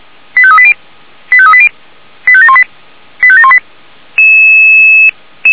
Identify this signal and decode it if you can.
Five-tone selective call (ZVEI), heard as quick runs of five tones, tones of about 70 ms. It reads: 85380 85380 85718 85718